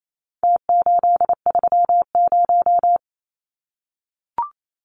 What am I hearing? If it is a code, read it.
Morse code, heard as a keyed tone, about 28 wpm, 700 Hz. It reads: T830
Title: Numbers - Contest Style - 4-Digit 28wpm